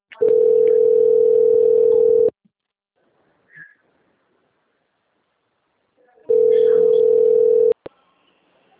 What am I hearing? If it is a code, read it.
Ringback tone (North America), 440 + 480 Hz.